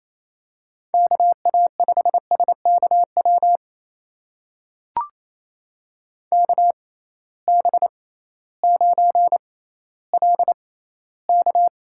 Morse code, heard as a keyed tone, about 28 wpm, spaced as for 15 wpm, 700 Hz. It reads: KA5SKW KB9LK
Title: USA Call Signs - 6 Characters - 6x Character Spacing - 28wpm